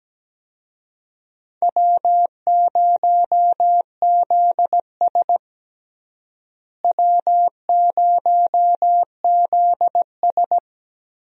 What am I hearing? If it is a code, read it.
Morse code, heard as a keyed tone, about 17 wpm, 700 Hz. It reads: W0ZS W0ZS
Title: USA Call Signs - 4 Characters - Repeated 3x - 17wpm